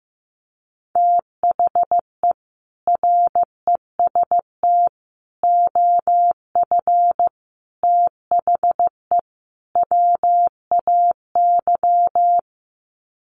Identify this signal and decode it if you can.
Morse code, heard as a keyed tone, about 15 wpm, 700 Hz. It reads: THE REST OF THE WAY
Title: Sets of 5 Words Encore 15wpm